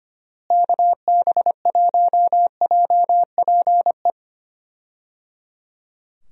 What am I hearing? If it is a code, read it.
Morse code, heard as a keyed tone, about 25 wpm, 700 Hz. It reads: KB1JPE
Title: USA Call Signs - 6 Characters - Repeated 3x - 25wpm